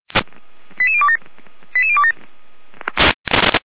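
Five-tone selective call (ZVEI), heard as quick runs of five tones, tones of about 70 ms. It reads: 88027 88027
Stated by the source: Five-tone ZVEI selective call received by SDR.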